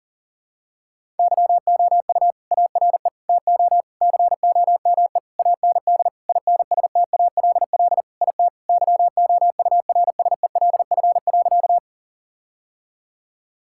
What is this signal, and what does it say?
Morse code, heard as a keyed tone, about 40 wpm, 700 Hz. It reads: YOU ARE TO COME AND INSTALL IT YOURSELF.
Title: Sentences from Top 1600 Words 40wpm